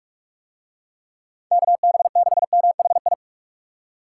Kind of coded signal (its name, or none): Morse code